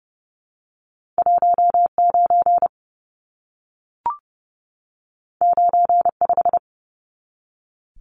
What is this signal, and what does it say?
Morse code, heard as a keyed tone, about 30 wpm, 700 Hz. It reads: 19 95
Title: Numbers - 2-Digit 30wpm